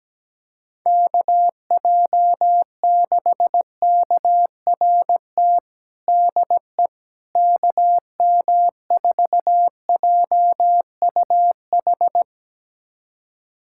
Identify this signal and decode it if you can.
Morse code, heard as a keyed tone, about 17 wpm, 700 Hz. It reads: KJ6KRT DE KM4JUH
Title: USA Callsign DE Callsign 17wpm